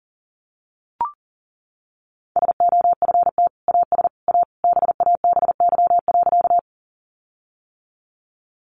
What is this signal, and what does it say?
Morse code, heard as a keyed tone, about 40 wpm, 700 Hz. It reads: SOFT AS A BABY.
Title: Sentences from Top 1500 Words 40wpm